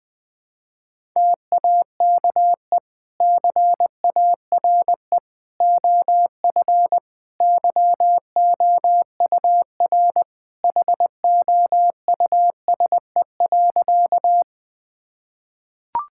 Morse code, heard as a keyed tone, about 20 wpm, 700 Hz. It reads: TAKE CARE OF YOUR HOUSE.
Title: Sentences with 2-Word Phrases - 20wpm